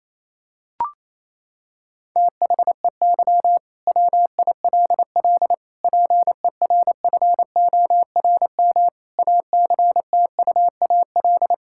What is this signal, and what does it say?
Morse code, heard as a keyed tone, about 28 wpm, 700 Hz. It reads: THEY WILL PERFORM ACTUAL
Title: Sentences from Top 1000 Words - Review Part 5 - 28wpm